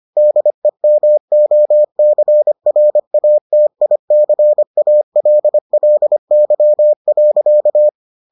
Morse code, heard as a keyed tone, about 25 wpm, 600 Hz. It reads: DEMOCRATICALLY.